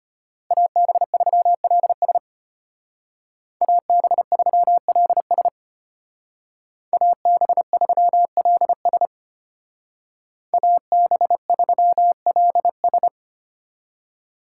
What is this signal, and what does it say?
Morse code, heard as a keyed tone, about 31 wpm, 700 Hz. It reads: AB3LS AB3LS AB3LS AB3LS
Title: Speed Racing - USA 2x2 Call Signs 25wpm